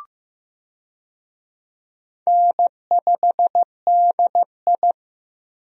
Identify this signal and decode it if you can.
Morse code, heard as a keyed tone, about 15 wpm, 700 Hz. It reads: N5DI
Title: USA Call Signs - 4 Characters - Repeated 2x - 15wpm